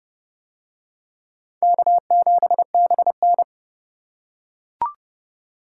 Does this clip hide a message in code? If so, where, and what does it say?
Yes: Morse code from 1.6 to 3.4 s, heard as a keyed tone, about 29 wpm, 700 Hz. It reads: K7BN